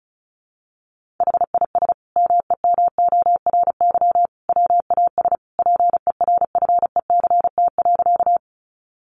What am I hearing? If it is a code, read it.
Morse code, heard as a keyed tone, about 34 wpm, 700 Hz. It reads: HIS MEMORY WAS PERFECT.
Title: Sentences from Top 1300 Words 35wpm